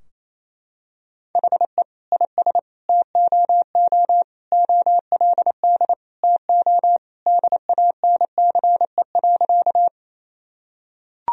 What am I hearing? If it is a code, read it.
Morse code, heard as a keyed tone, about 28 wpm, 700 Hz. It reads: HE IS TOO OLD TO DANCE.
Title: Sentences from Top 1000 Words - Review Part 4 - 28wpm